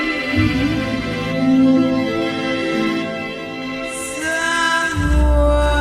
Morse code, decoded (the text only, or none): none